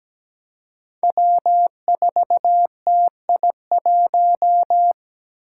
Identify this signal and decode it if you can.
Morse code, heard as a keyed tone, about 17 wpm, 700 Hz. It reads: W4TI1
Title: Five Challenging Letter-Number Combinations 17wpm